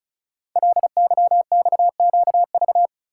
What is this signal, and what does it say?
Morse code, heard as a keyed tone, about 36 wpm, 700 Hz. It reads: LYXQV
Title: Code Groups - 5 Letters - 35wpm